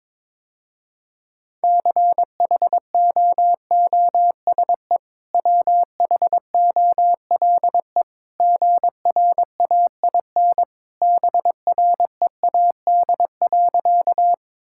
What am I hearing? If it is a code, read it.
Morse code, heard as a keyed tone, about 22 wpm, 700 Hz. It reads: CHOOSE WHOLE GRAIN BREAD.
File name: practice / 22wpm / Easy Sentences with Hints - 22wpm